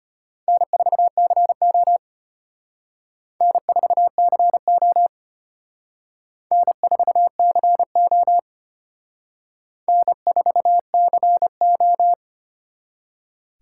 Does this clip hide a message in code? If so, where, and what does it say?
Yes: Morse code from 0.5 to 12.1 s, heard as a keyed tone, about 32 wpm, 700 Hz. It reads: N4CO N4CO N4CO N4CO